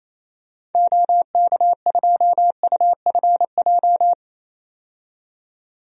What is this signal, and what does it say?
Morse code, heard as a keyed tone, about 28 wpm, 700 Hz. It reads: OK2UFJ
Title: DXCC-100 Call Signs - 6 Characters - 28wpm